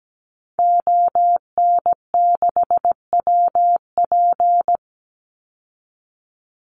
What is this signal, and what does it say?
Morse code, heard as a keyed tone, about 17 wpm, 700 Hz. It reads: ON6WP
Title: Call Signs - 17wpm